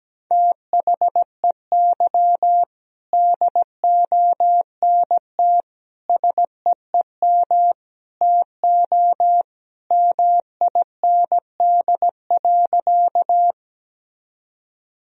Morse code, heard as a keyed tone, about 17 wpm, 700 Hz. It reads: THEY DONT SEEM TO MIND.
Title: Sentences from Top 1000 Words - Review Part 1 - 17wpm